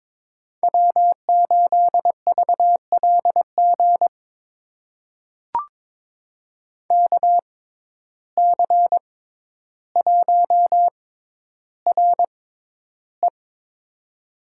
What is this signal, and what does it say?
Morse code, heard as a keyed tone, about 21 wpm, spaced as for 11 wpm, 700 Hz. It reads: W8VLG KC1RE
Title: USA Call Signs - 5 Characters - 6x Character Spacing - 22wpm